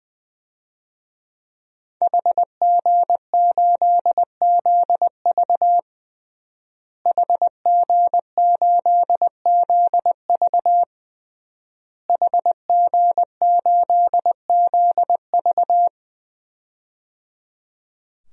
Morse code, heard as a keyed tone, about 21 wpm, 700 Hz. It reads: HG8ZV HG8ZV HG8ZV
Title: DXCC-100 Call Signs - 5 Characters - Repeated 3x - 20wpm